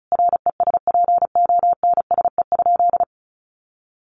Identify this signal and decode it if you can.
Morse code, heard as a keyed tone, about 34 wpm, 700 Hz. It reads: RESPONSE?
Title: Sentences from Top 1400 Words 35wpm